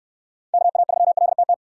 Morse code, heard as a keyed tone, about 65 wpm, 700 Hz. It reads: KA3LTT